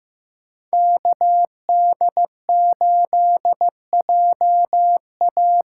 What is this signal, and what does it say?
Morse code, heard as a keyed tone, about 15 wpm, 700 Hz. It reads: KD8JA
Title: USA Call Signs - 6x Character Spacing - 15wpm